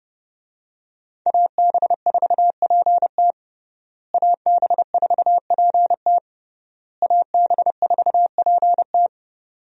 Morse code, heard as a keyed tone, about 30 wpm, 700 Hz. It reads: AB4PT AB4PT AB4PT
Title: USA Call Signs - 5 Characters - Repeated 3x - 30wpm